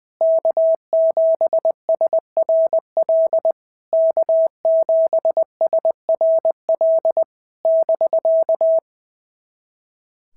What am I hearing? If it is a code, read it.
Morse code, heard as a keyed tone, about 19 wpm, 650 Hz. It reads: K7SRL K7SRL <BK>
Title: SOTA QSO Practice 20wpm